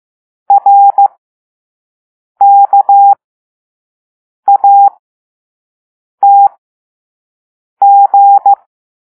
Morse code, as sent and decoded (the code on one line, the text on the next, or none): .-. -.- .- - --.
RKATG